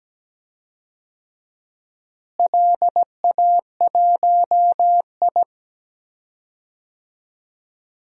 Morse code, heard as a keyed tone, about 17 wpm, 700 Hz. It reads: LA1I
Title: DXCC-100 Call Signs - 4 Characters - Repeated 2x - 17wpm